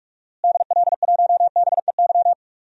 Morse code, heard as a keyed tone, about 45 wpm, 700 Hz. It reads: DL1BEY
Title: DXCC-100 Call Signs - 6 Characters - Repeated 2x - 45wpm